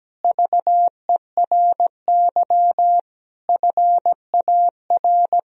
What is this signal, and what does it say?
Morse code, heard as a keyed tone, about 17 wpm, 700 Hz. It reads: VERY FAR
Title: Sentences from Top 1000 Words - Review Part 2 - 17wpm